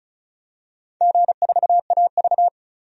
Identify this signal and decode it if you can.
Morse code, heard as a keyed tone, about 35 wpm, 700 Hz. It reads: G4AV
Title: DXCC-100 Call Signs - 4 Characters - 35wpm